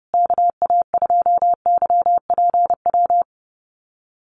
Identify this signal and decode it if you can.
Morse code, heard as a keyed tone, about 30 wpm, 700 Hz. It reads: KA2YPW
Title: USA Call Signs Repeated 3x 30wpm